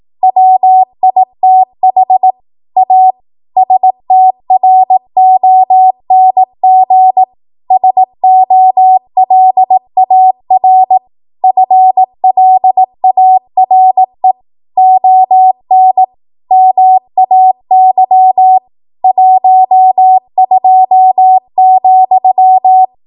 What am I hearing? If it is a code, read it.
Morse code, heard as a keyed tone, about 18 wpm, 750 Hz. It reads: WITH A STRONG SOLAR FLARE ON MAY 12,
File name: Morse / 18WPM